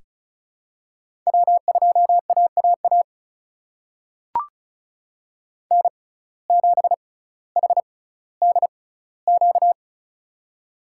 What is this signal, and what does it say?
Morse code, heard as a keyed tone, about 36 wpm, spaced as for 18 wpm, 700 Hz. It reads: W2AAA N7HDQ
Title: USA Call Signs - 5 Characters - 6x Character Spacing - 35wpm